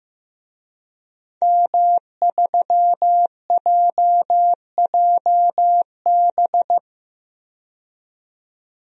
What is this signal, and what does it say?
Morse code, heard as a keyed tone, about 15 wpm, 700 Hz. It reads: M3JJB
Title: DXCC-100 Call Signs - 5 Characters - 15wpm